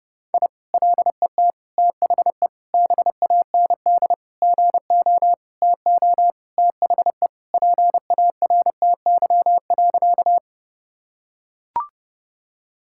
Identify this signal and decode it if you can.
Morse code, heard as a keyed tone, about 30 wpm, 700 Hz. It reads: I LET THE BAND GO TO THE PARTY.
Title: Sentences from Top 1600 Words 30wpm